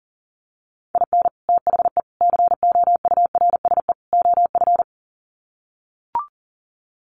Morse code, heard as a keyed tone, about 40 wpm, 700 Hz. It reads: IN THE COURSE OF
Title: Sets of 4 Words 40wpm